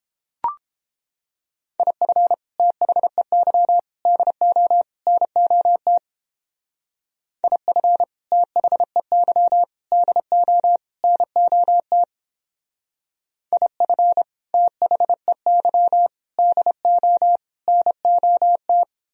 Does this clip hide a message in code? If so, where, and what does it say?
Yes: Morse code from 1.8 to 18.8 s, heard as a keyed tone, about 31 wpm, 700 Hz. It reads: IF THEY DO NOT IF THEY DO NOT IF THEY DO NOT